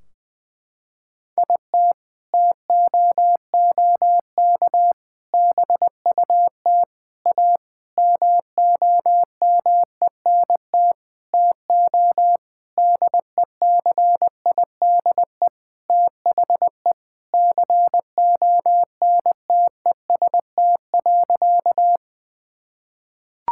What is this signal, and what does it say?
Morse code, heard as a keyed tone, about 20 wpm, 700 Hz. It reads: IT TOOK BUT A MOMENT TO DECIDE THE CONTEST.
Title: Sentences from Top 1600 Words 20wpm